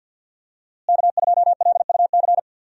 Morse code, heard as a keyed tone, about 52 wpm, 700 Hz. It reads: K2LUC
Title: USA Call Signs - 5 Characters - Repeated 3x - 50wpm